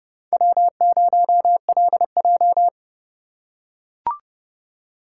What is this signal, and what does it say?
Morse code, heard as a keyed tone, about 29 wpm, 700 Hz. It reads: W0LJ